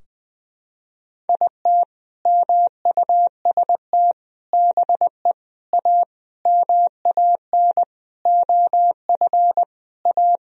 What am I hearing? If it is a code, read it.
Morse code, heard as a keyed tone, about 20 wpm, 700 Hz. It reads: IT MUST BE A MAN OF A
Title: Sentences from Top 1000 Words - Review Part 3 - 20wpm